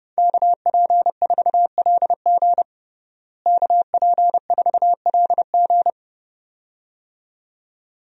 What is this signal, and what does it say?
Morse code, heard as a keyed tone, about 32 wpm, 700 Hz. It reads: KP4LG KP4LG
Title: DXCC-100 Call Signs - 5 Characters - Repeated 2x - 30wpm